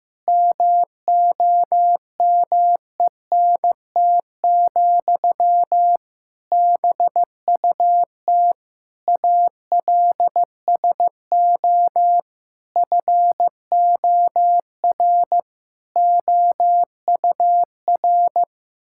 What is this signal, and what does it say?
Morse code, heard as a keyed tone, about 16 wpm, 700 Hz. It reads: MOMENT, BUT ALSO FOR OUR
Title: Sentences from Top 1000 Words - Review Part 6 - 15wpm